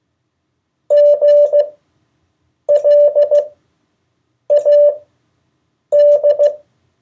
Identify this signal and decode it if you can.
Morse code, heard as a keyed tone, about 15 wpm, spaced as for 7 wpm, 580 Hz. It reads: GLAD
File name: 2of4-words-1